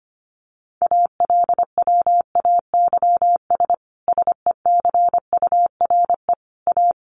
Morse code, heard as a keyed tone, about 25 wpm, 700 Hz. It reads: ALWAYS SECURE A
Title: Sentences from Top 1200 Words 25wpm